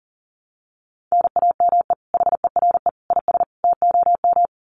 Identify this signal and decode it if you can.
Morse code, heard as a keyed tone, about 40 wpm, 700 Hz. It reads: NAME HERE IS TOM
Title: Common QSO Phrases 40wpm